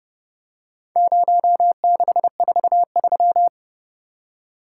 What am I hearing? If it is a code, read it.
Morse code, heard as a keyed tone, about 29 wpm, 700 Hz. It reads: 0643